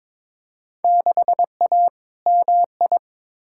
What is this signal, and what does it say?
Morse code, heard as a keyed tone, about 22 wpm, 700 Hz. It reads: 6A MI
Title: Field Day Exchanges - 22wpm